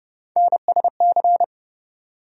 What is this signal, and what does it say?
Morse code, heard as a keyed tone, about 30 wpm, 700 Hz. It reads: NSC
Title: USA Call Sign Postfixes - 3 Letters 30wpm